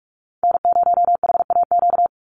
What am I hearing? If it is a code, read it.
Morse code, heard as a keyed tone, about 44 wpm, 700 Hz. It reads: N0HAQ
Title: USA Call Signs 45wpm